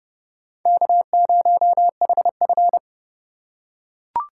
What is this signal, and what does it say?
Morse code, heard as a keyed tone, about 28 wpm, 700 Hz. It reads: K0HF